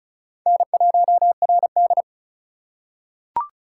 Morse code, heard as a keyed tone, about 34 wpm, 700 Hz. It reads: N1RD
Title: USA Call Signs - 4 Characters - 4x Character Spacing - 35wpm